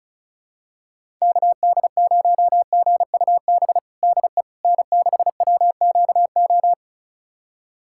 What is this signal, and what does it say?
Morse code, heard as a keyed tone, about 34 wpm, 700 Hz. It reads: KD0GUB DE N6WQO
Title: USA Callsign DE Callsign 35wpm